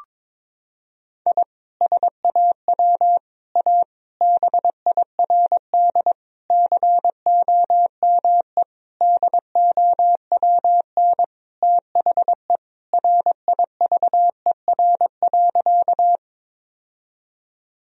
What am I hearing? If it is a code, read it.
Morse code, heard as a keyed tone, about 22 wpm, 700 Hz. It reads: I SAW A BIRD COME DOWN THE RIVER.
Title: Sentences from Top 1000 Words - Review Part 2 - 22wpm